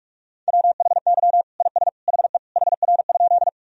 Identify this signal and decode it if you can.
Morse code, heard as a keyed tone, about 47 wpm, 700 Hz. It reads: WHY IS HE HR?
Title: Sentences from Top 200 Words and Abbreviations 45wpm